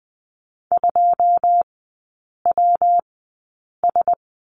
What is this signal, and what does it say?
Morse code, heard as a keyed tone, about 20 wpm, 700 Hz. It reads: 2 W S
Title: 114 Lesson 19e - Letter w - Three Characters 20wpm